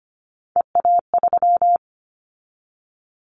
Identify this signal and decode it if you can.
Morse code, heard as a keyed tone, about 25 wpm, 700 Hz. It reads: EA3
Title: Callsign Prefixes for Top 25 DXCC - 25wpm